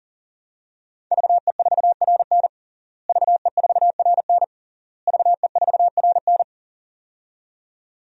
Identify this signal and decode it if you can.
Morse code, heard as a keyed tone, about 40 wpm, 700 Hz. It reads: VE4RN VE4RN VE4RN
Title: DXCC-100 Call Signs - 5 Characters - Repeated 3x - 40wpm